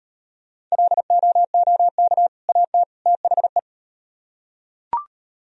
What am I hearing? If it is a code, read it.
Morse code, heard as a keyed tone, about 35 wpm, 700 Hz. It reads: LOOK AT THE